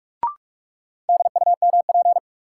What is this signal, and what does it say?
Morse code, heard as a keyed tone, about 46 wpm, 700 Hz. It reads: DUMP
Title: Sentences from Top 1300 Words - New Word Drill Alt - 45wpm